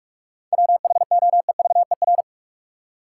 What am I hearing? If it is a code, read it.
Morse code, heard as a keyed tone, about 46 wpm, 700 Hz. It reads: WHOEVER
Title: Seven Letter Words - 45wpm